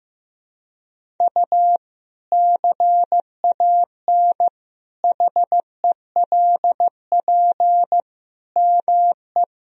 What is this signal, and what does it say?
Morse code, heard as a keyed tone, about 16 wpm, 700 Hz. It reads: U CAN HELP ME
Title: Sentences from Top 200 Words and Abbreviations 15wpm